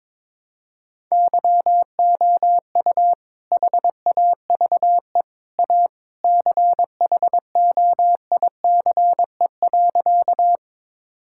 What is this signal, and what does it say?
Morse code, heard as a keyed tone, about 21 wpm, 700 Hz. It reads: YOU HAVE A CHOICE.
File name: practice / 22wpm / Sentences with 3-Word Phrases - 22wpm